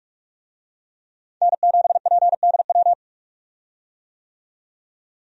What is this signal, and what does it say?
Morse code, heard as a keyed tone, about 42 wpm, 700 Hz. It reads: N7PDW
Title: USA Call Signs - 5 Characters - 45wpm